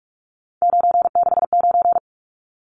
Morse code, heard as a keyed tone, about 44 wpm, 700 Hz. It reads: 969
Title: Numbers - 3-Digit 45wpm